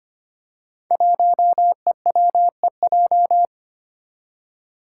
Morse code, heard as a keyed tone, about 25 wpm, 700 Hz. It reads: 1EWEJ
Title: Five Challenging Letter-Number Combinations 25wpm